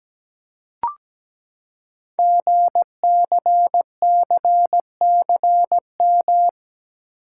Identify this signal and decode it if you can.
Morse code, heard as a keyed tone, about 17 wpm, 700 Hz. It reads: GCCCM
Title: Code Groups - 5 Letters - 17wpm